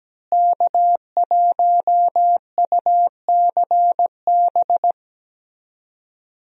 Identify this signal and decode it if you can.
Morse code, heard as a keyed tone, about 17 wpm, 700 Hz. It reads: K1UCB
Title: USA Call Signs - 5 Characters - Repeated 2x - 17wpm